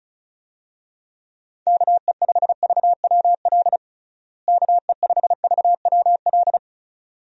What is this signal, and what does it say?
Morse code, heard as a keyed tone, about 36 wpm, 700 Hz. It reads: KE5VWL KE5VWL
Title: USA Call Signs - 6 Characters - Repeated 2x - 35wpm